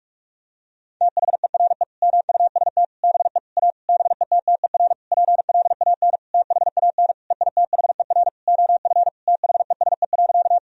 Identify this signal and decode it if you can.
Morse code, heard as a keyed tone, about 46 wpm, 700 Hz. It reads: THERE MUST BE A BETTER PLAN THAN EITHER OF THESE.
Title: Sentences from Top 1000 Words - Review Part 4 - 45wpm